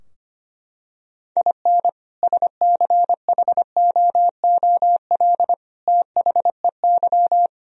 Morse code, heard as a keyed tone, about 25 wpm, 700 Hz. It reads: IN SCHOOL THEY
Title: Sentences from Top 1000 Words - Review Part 4 - 25wpm